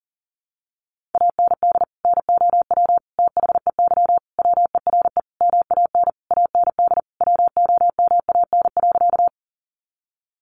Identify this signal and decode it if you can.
Morse code, heard as a keyed tone, about 40 wpm, 700 Hz. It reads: AND NOW THEY WERE MAN AND WOMAN.
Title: Sentences from Top 1100 Words 40wpm